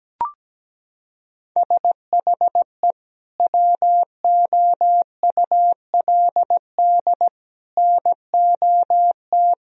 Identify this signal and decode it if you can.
Morse code, heard as a keyed tone, about 17 wpm, 700 Hz. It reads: SHE WOULD NOT
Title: Sentences from Top 1000 Words - Review Part 2 - 17wpm